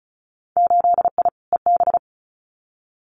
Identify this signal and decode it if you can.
Morse code, heard as a keyed tone, about 36 wpm, 700 Hz. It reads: 8I EB
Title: Winter Field Day Exchanges Encore 35wpm